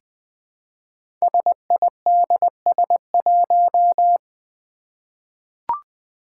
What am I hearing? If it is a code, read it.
Morse code, heard as a keyed tone, about 20 wpm, 700 Hz. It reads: SIDS1
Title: Five Challenging Letter-Number Combinations 20wpm